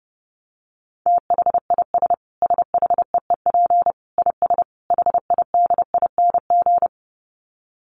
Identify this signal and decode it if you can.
Morse code, heard as a keyed tone, about 30 wpm, 700 Hz. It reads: THIS SHEEP IS HIDING
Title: Dit Challenge 30wpm